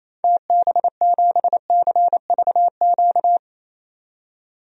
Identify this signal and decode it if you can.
Morse code, heard as a keyed tone, about 28 wpm, 700 Hz. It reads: TB7CVQ
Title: DXCC-100 Call Signs - 6 Characters - 28wpm